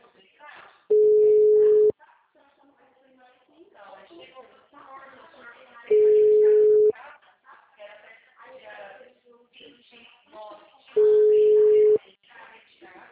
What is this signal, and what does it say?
Ringback tone (Europe, 425 Hz), with speech.